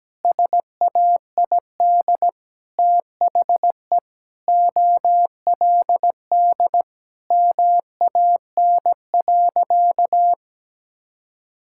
Morse code, heard as a keyed tone, about 17 wpm, 700 Hz. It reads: SAID THE OLD MAN.
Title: Sentences from Top 1600 Words 17wpm